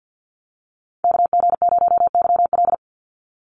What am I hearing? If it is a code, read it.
Morse code, heard as a keyed tone, about 48 wpm, 700 Hz. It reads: KG0YL
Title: Call Signs Repeated 2x - 50wpm